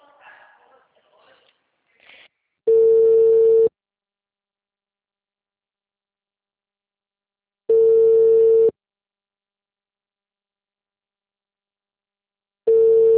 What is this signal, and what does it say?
Ringback tone (China, 450 Hz), with speech.
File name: out- -2011- - - .1657